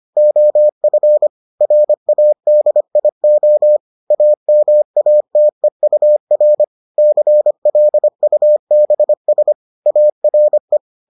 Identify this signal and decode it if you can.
Morse code, heard as a keyed tone, about 25 wpm, 600 Hz. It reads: OF RADIO AMATEUR CLUBS ARE